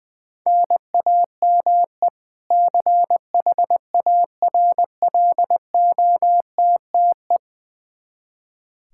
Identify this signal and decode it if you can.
Morse code, heard as a keyed tone, about 20 wpm, 700 Hz. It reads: NAME CHARLOTTE
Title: Practice QSO - Exchange 1 - Beginning - 20wpm